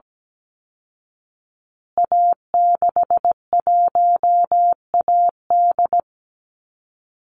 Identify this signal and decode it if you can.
Morse code, heard as a keyed tone, about 17 wpm, 700 Hz. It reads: A61AD
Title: Call Signs Repeated 2x - 17wpm